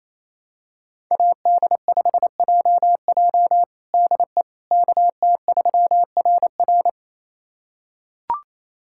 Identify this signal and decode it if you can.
Morse code, heard as a keyed tone, about 27 wpm, 700 Hz. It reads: AD5JJ DE KT3RR
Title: USA Callsign DE Callsign - 5 Characters - 28wpm